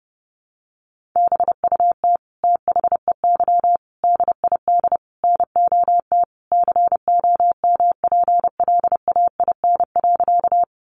Morse code, heard as a keyed tone, about 30 wpm, 700 Hz. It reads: BUT THEY DID NOT COMPLAIN.
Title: Sentences from Top 1300 Words 30wpm